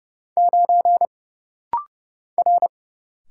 Morse code, heard as a keyed tone, about 30 wpm, 700 Hz. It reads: 9 R
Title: Single Letter-Number Alt - Rapid-Fire 30wpm